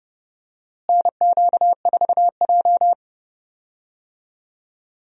Morse code, heard as a keyed tone, about 30 wpm, 700 Hz. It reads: NQ4J
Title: USA Call Signs - 4 Characters - Repeated 3x - 30wpm